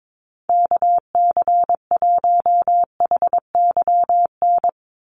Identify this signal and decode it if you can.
Morse code, heard as a keyed tone, about 22 wpm, 700 Hz. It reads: KC1HYN
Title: USA Call Signs 22wpm